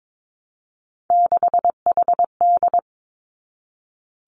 Morse code, heard as a keyed tone, about 22 wpm, 700 Hz. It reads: 6HD
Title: Three Challenging Letter-Number Combinations 22wpm